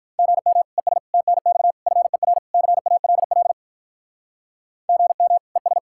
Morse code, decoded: GM ES TNX FER CALL GM ES

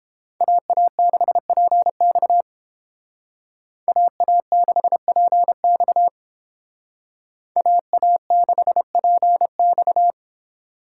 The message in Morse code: AA6PX AA6PX AA6PX

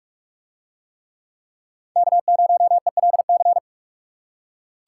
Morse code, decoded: K0ELC

